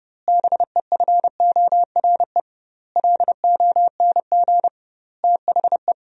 Morse code, heard as a keyed tone, about 30 wpm, 700 Hz. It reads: BEFORE LONG THE
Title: 2x Word-Spacing - Sentences from Top 400 Words - 30wpm